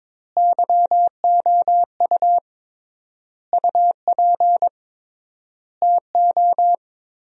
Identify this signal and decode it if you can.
Morse code, heard as a keyed tone, about 22 wpm, 700 Hz. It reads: YOU UP TO
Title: 3x Word-Spacing - Sentences from Top 100 Words - 22wpm